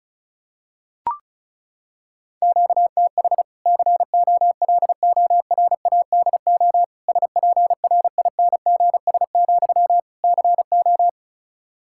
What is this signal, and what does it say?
Morse code, heard as a keyed tone, about 34 wpm, 700 Hz. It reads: QTH COLORADO SPRINGS, CO
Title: Practice QSO - Exchange 1 - Beginning - 35wpm